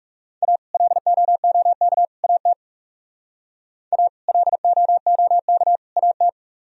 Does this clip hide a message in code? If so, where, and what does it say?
Yes: Morse code from 0.4 to 6.3 s, heard as a keyed tone, about 39 wpm, 700 Hz. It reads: A LOOK AT A LOOK AT